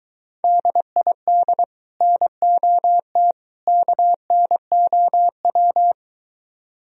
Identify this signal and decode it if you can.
Morse code, heard as a keyed tone, about 23 wpm, 700 Hz. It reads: DID NOT KNOW